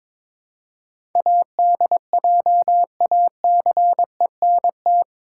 Morse code, heard as a keyed tone, about 22 wpm, 700 Hz. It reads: ADJACENT
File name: practice / 22wpm / Eight Letter Words - 22wpm